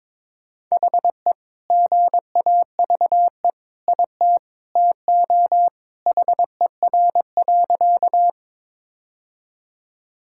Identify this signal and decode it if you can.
Morse code, heard as a keyed tone, about 22 wpm, 700 Hz. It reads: HE GAVE IT TO HER.
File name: practice / 22wpm / Sentences with 2-Word Phrases - 22wpm